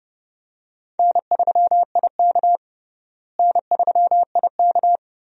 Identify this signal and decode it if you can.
Morse code, heard as a keyed tone, about 30 wpm, 700 Hz. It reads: N3IK N3IK
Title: USA Call Signs - 4 Characters - Repeated 3x - 30wpm